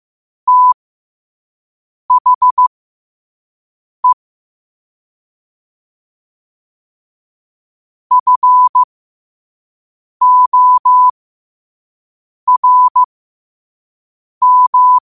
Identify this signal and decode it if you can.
Morse code, heard as a keyed tone, about 15 wpm, spaced as for 5 wpm, 1000 Hz. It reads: THE FORM
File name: qotdfiles / 5wpm